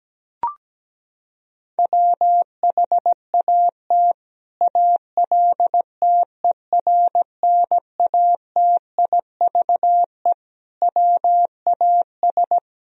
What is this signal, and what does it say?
Morse code, heard as a keyed tone, about 17 wpm, 700 Hz. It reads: WHAT ALTERNATIVE WAS
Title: Sentences from Top 1000 Words - Review Part 4 - 17wpm